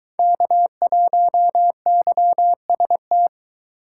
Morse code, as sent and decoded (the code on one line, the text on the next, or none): -.- .---- -.-- ... -
K1YST